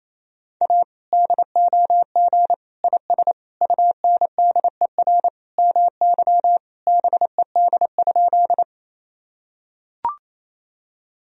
Morse code, heard as a keyed tone, about 28 wpm, 700 Hz. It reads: A DOG IS UNDER MY BED?
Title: Sentences from Top 1000 Words - Review Part 2 - 28wpm